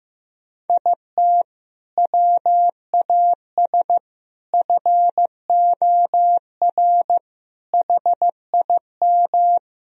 Morse code, heard as a keyed tone, about 15 wpm, 700 Hz. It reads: IT WAS FOR HIM
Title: Sentences from Top 1000 Words - Review Part 1 - 15wpm